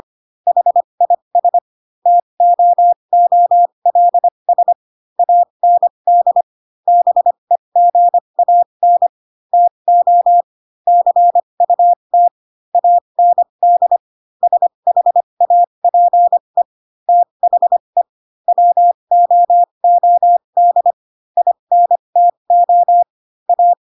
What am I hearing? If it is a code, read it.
Morse code, heard as a keyed tone, about 25 wpm, spaced as for 22 wpm, 700 Hz. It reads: HIS TOOLS AND BEGAN TO CUT AND SHAPE THE WOOD INTO A
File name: Pinocchio / Pinocchio ch3 18 25 ik1hge